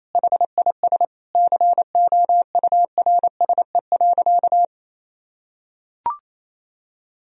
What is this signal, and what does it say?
Morse code, heard as a keyed tone, about 27 wpm, 700 Hz. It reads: HIS COURSE.
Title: Sentences from Top 1000 Words - Review Part 2 - 28wpm